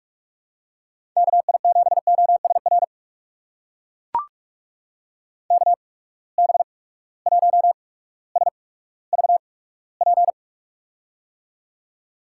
Morse code, heard as a keyed tone, about 42 wpm, spaced as for 19 wpm, 700 Hz. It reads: KI7OSR KB1SVP
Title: USA Call Signs - 6 Characters - 8x Character Spacing - 45wpm